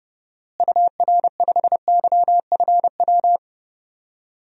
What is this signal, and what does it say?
Morse code, heard as a keyed tone, about 30 wpm, 700 Hz. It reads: UR5YFW
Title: DXCC-100 Call Signs - 6 Characters - Repeated 2x - 30wpm